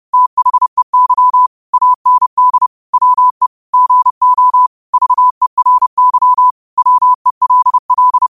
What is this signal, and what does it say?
Morse code, heard as a keyed tone, about 30 wpm, 1000 Hz. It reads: THEY AND WE GO VERY WELL